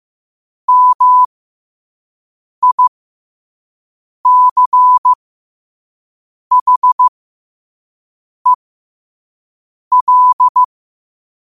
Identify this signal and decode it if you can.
Morse code, heard as a keyed tone, about 15 wpm, spaced as for 5 wpm, 1000 Hz. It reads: MICHEL
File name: qotdfiles / 5wpm